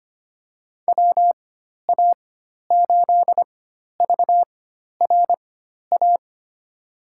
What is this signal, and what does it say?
Morse code, heard as a keyed tone, about 25 wpm, spaced as for 12 wpm, 700 Hz. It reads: WA8VRA
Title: USA Call Signs - 6 Characters - 4x Character Spacing - 25wpm